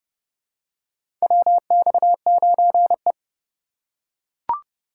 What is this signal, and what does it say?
Morse code, heard as a keyed tone, about 30 wpm, 700 Hz. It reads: WX9E
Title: USA Call Signs - 4 Characters - 8x Character Spacing - 30wpm